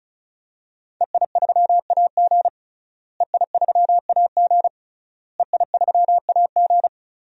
Morse code, heard as a keyed tone, about 36 wpm, 700 Hz. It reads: EI3AG EI3AG EI3AG
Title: DXCC-100 Call Signs - 5 Characters - Repeated 3x - 35wpm